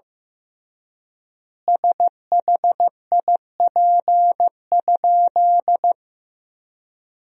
Morse code, heard as a keyed tone, about 15 wpm, 700 Hz. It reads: SHIP?
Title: 3x Word-Spacing - Sentences from Top 400 Words - 15wpm